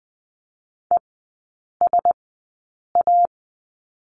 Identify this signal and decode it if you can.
Morse code, heard as a keyed tone, about 20 wpm, spaced as for 8 wpm, 700 Hz. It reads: ESA
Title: 093 Lesson 16e - Letter c - Three Characters 20wpm